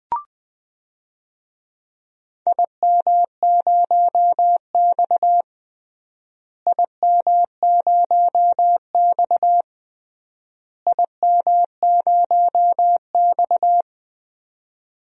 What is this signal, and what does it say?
Morse code, heard as a keyed tone, about 20 wpm, 700 Hz. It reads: IM0X IM0X IM0X
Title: DXCC-100 Call Signs - 4 Characters - Repeated 3x - 20wpm